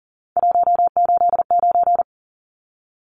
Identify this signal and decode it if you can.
Morse code, heard as a keyed tone, about 40 wpm, 700 Hz. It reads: 189
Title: Numbers - 3-Digit 40wpm